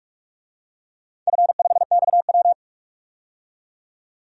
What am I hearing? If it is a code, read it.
Morse code, heard as a keyed tone, about 42 wpm, 700 Hz. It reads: F5XW
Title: DXCC-100 Call Signs - 4 Characters - 45wpm